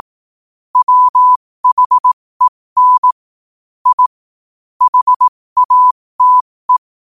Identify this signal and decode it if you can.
Morse code, heard as a keyed tone, about 18 wpm, spaced as for 15 wpm, 1000 Hz. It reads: WHEN I HATE
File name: qotdfiles / 15wpm